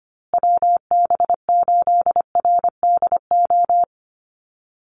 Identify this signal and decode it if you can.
Morse code, heard as a keyed tone, about 25 wpm, 700 Hz. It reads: WB8RDO
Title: USA Call Signs Repeated 2x 25wpm